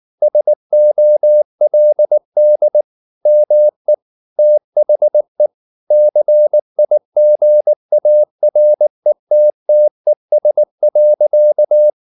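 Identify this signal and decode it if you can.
Morse code, heard as a keyed tone, about 19 wpm, 600 Hz. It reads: SOLD ME THE CIGARETTES.
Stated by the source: Morse code at 19 wpm from CW med Gnister, reading Ken Follett's novel Never in English.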